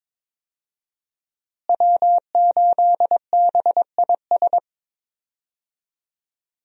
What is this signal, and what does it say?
Morse code, heard as a keyed tone, about 22 wpm, 700 Hz. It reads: W8BIS
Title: USA Call Signs - 5 Characters - Repeated 3x - 22wpm